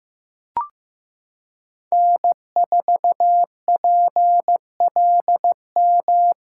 Morse code, heard as a keyed tone, about 15 wpm, 700 Hz. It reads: N4PLM
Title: USA Callsign DE Callsign 15wpm